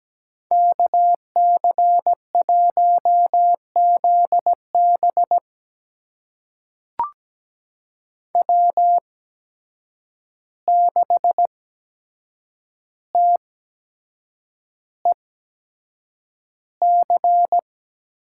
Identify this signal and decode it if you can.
Morse code, heard as a keyed tone, about 17 wpm, spaced as for 7 wpm, 700 Hz. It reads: KC1ZB W6TEC
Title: USA Call Signs - 5 Characters - 8x Character Spacing - 17wpm